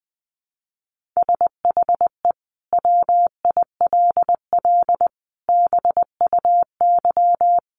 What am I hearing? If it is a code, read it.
Morse code, heard as a keyed tone, about 20 wpm, 700 Hz. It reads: SHE WILL BUY
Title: Sentences from Top 1500 Words 20wpm